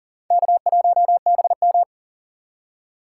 Morse code, heard as a keyed tone, about 40 wpm, 700 Hz. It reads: K1BM
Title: USA Call Signs - 4 Characters - 4x Character Spacing - 40wpm